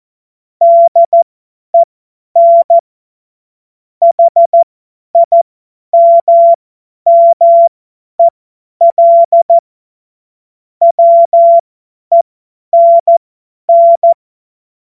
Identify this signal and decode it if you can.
Morse code, heard as a keyed tone, about 14 wpm, spaced as for 10 wpm, 680 Hz. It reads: DEN HIMMEL WENN
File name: Morsekurs / Witze / Witz 02. Februar 2021, 50 F70